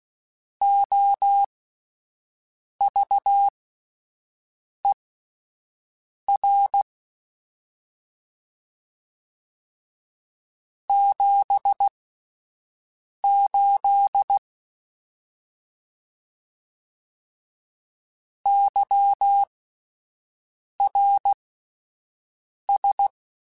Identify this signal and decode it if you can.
Morse code, heard as a keyed tone, about 16 wpm, spaced as for 5 wpm, 800 Hz. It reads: OVER 78 YRS
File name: QSOs Spaced